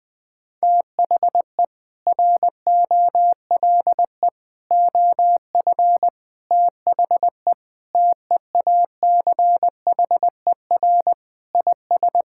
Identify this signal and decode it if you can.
Morse code, heard as a keyed tone, about 20 wpm, 700 Hz. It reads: THE ROLE OF THE TEACHER IS
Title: Sentences with 4-Word Phrases - 20wpm